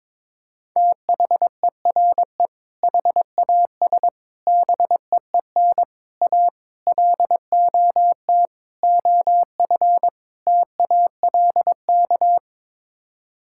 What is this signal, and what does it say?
Morse code, heard as a keyed tone, about 21 wpm, 700 Hz. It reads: THERE HAS BEEN A LOT OF TALK
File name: practice / 22wpm / Sets of 7 Words - 22wpm